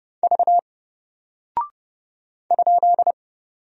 Morse code, heard as a keyed tone, about 29 wpm, 700 Hz. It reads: V ?
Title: Single Character (Letter:Number:Punctuation:Prosign) 30wpm